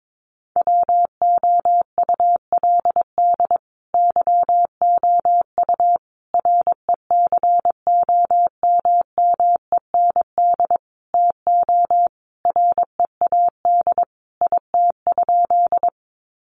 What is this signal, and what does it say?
Morse code, heard as a keyed tone, about 22 wpm, 700 Hz. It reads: WOULD YOU RECOMMEND TO READ IT?
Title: Sentences from Top 1200 Words 22wpm